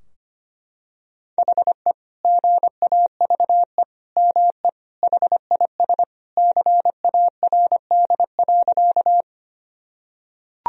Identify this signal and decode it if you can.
Morse code, heard as a keyed tone, about 25 wpm, 700 Hz. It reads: HE GAVE ME HIS CARD.
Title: Sentences from Top 1000 Words - Review Part 4 - 25wpm